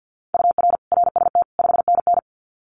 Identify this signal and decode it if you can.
Morse code, heard as a keyed tone, about 50 wpm, 700 Hz. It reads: UR RST 5NN
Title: Common QSO Phrases Encore 50wpm